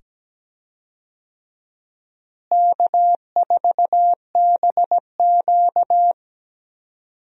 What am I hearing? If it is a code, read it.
Morse code, heard as a keyed tone, about 17 wpm, 700 Hz. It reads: K4BQ
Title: USA Call Signs - 4 Characters - Repeated 2x - 17wpm